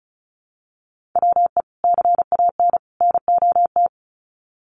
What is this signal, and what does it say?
Morse code, heard as a keyed tone, about 34 wpm, 700 Hz. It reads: WE CAN NOT
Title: Sets of 3 Words 35wpm